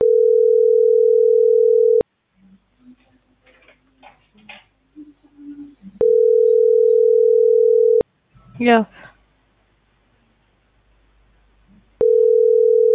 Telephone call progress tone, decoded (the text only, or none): ringback tone